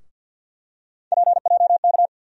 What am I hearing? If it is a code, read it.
Morse code, heard as a keyed tone, about 53 wpm, 700 Hz. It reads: PJK
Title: USA Call Sign Postfixes - 3 Letters 50wpm